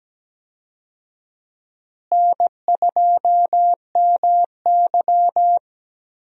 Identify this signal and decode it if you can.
Morse code, heard as a keyed tone, about 17 wpm, 700 Hz. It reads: N2MY
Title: USA Call Signs - 4 Characters - Repeated 2x - 17wpm